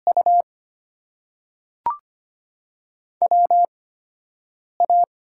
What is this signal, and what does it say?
Morse code, heard as a keyed tone, about 25 wpm, spaced as for 7 wpm, 700 Hz. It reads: U WA
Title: USA Call Signs - 6 Characters - 8x Character Spacing - 25wpm